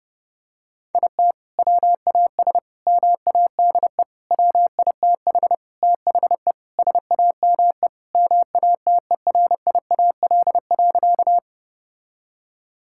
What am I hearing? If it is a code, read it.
Morse code, heard as a keyed tone, about 30 wpm, 700 Hz. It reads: IT WAS MADE WITH THE SAME MATERIAL.
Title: Sentences from Top 1000 Words - Review Part 5 - 30wpm